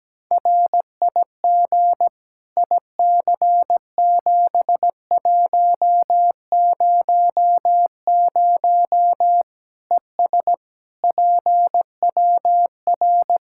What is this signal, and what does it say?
Morse code, heard as a keyed tone, about 17 wpm, 700 Hz. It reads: RIG IC7100 ES PWR
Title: K1USN SST QSO Practice 17wpm